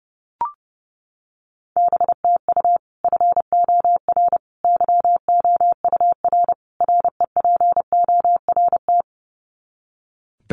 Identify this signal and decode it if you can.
Morse code, heard as a keyed tone, about 30 wpm, 700 Hz. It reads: BTU FOR YOUR REPORT
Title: Common QSO Phrases 30wpm